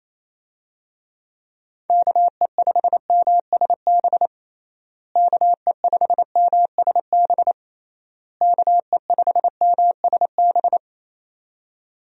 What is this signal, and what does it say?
Morse code, heard as a keyed tone, about 28 wpm, 700 Hz. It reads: KE5MSB KE5MSB KE5MSB
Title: USA Call Signs - 6 Characters - Repeated 3x - 28wpm